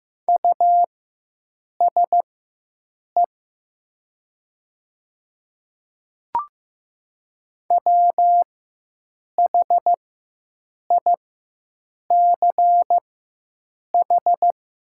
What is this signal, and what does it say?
Morse code, heard as a keyed tone, about 15 wpm, spaced as for 7 wpm, 700 Hz. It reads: USE WHICH
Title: 4x Character Spacing - Top 100 Words - 15wpm